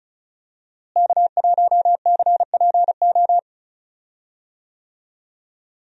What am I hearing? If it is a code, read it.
Morse code, heard as a keyed tone, about 36 wpm, 700 Hz. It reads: K1CPO
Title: USA Call Signs - 5 Characters - Repeated 2x - 35wpm